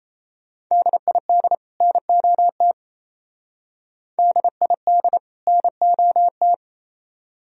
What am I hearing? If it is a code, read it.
Morse code, heard as a keyed tone, about 35 wpm, 700 Hz. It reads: DID NOT DID NOT
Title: Speed-Racing - Sets of 2 Words 28wpm